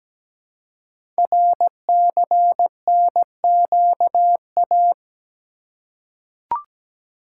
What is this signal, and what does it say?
Morse code, heard as a keyed tone, about 17 wpm, 700 Hz. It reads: RCNQA
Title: Code Groups - 5 Letters - 17wpm